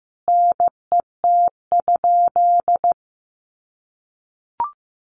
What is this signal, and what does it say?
Morse code, heard as a keyed tone, about 15 wpm, 700 Hz. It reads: NET?
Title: Sentences from Top 1400 Words 15wpm